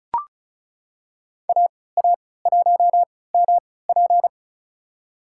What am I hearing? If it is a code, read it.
Morse code, heard as a keyed tone, about 34 wpm, spaced as for 20 wpm, 700 Hz. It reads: AA1MP
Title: USA Call Signs - 5 Characters - 3x Character Spacing - 35wpm